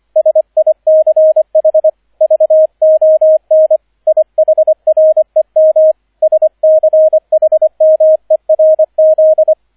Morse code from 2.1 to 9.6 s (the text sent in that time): VON IHREM SCHMERZ